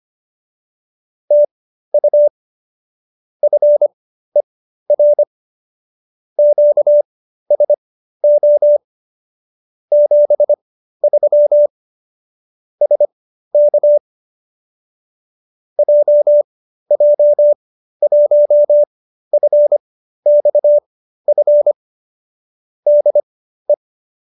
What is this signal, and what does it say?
Morse code, heard as a keyed tone, about 25 wpm, spaced as for 13 wpm, 590 Hz. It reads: TU FER QSO 73 SK JJ1FXF DE
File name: qso208b 13